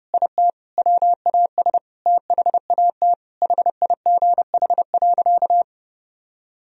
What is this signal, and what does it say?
Morse code, heard as a keyed tone, about 30 wpm, 700 Hz. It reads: IT WAS THAT HIGH.
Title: Sentences from Top 1000 Words - Review Part 1 - 30wpm